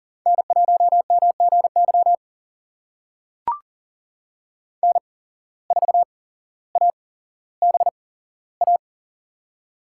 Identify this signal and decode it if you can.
Morse code, heard as a keyed tone, about 37 wpm, spaced as for 17 wpm, 700 Hz. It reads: N1MGY N4ABA
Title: USA Call Signs - 5 Characters - 8x Character Spacing - 40wpm